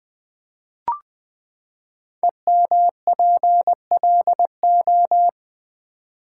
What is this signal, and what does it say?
Morse code, heard as a keyed tone, about 20 wpm, 700 Hz. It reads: EMPLO
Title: Build A Word - 5 Letters At A Time - 10-Letter Words - 20wpm